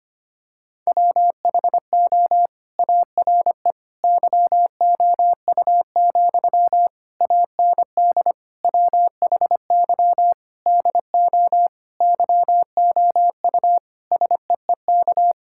Morse code, heard as a keyed tone, about 25 wpm, 700 Hz. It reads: WHO ARE YOU, AND WHY DO YOU SEEK
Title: Sentences from Top 1600 Words 25wpm